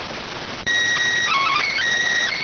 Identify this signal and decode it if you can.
SSTV (slow-scan television), PD 290.